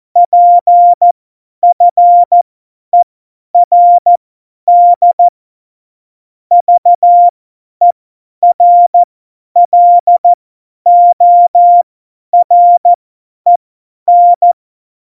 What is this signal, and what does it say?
Morse code, heard as a keyed tone, about 14 wpm, spaced as for 10 wpm, 700 Hz. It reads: PFERD VERLOREN
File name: Morsekurs / Witze / Witz 11. Juni 2019, 50 F70